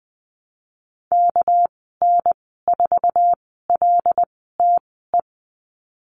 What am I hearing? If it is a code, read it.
Morse code, heard as a keyed tone, about 20 wpm, 700 Hz. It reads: K N 4 L T E
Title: 231 Lesson 37g - Letter x - Call Signs 20wpm